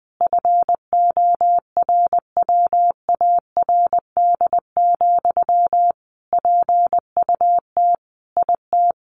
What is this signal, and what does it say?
Morse code, heard as a keyed tone, about 20 wpm, 700 Hz. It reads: FORWARD, PUT IT
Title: Sentences from Top 1500 Words 20wpm